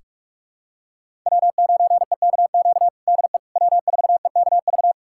Morse code, heard as a keyed tone, about 42 wpm, 700 Hz. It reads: W9EKQ DE W4EKV
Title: USA Callsign DE Callsign - 5 Characters - 45wpm